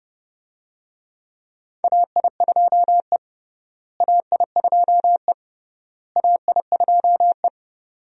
Morse code, heard as a keyed tone, about 30 wpm, 700 Hz. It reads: AI2E AI2E AI2E
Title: USA Call Signs - 4 Characters - Repeated 3x - 30wpm